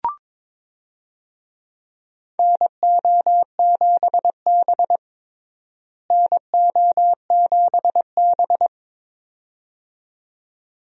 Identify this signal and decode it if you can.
Morse code, heard as a keyed tone, about 22 wpm, 700 Hz. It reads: NO7B NO7B
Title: USA Call Signs - 4 Characters - Repeated 2x - 22wpm